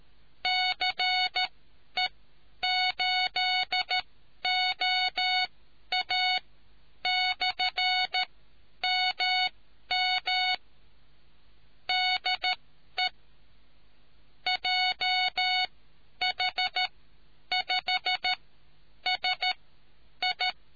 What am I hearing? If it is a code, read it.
Morse code, heard as a keyed tone, about 13 wpm, spaced as for 9 wpm, 740 Hz. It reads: CE8OA/MM DE JH5SI